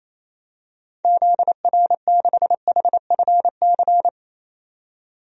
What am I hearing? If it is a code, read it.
Morse code, heard as a keyed tone, about 28 wpm, 700 Hz. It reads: ZR6HFC
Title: DXCC-100 Call Signs - 6 Characters - 28wpm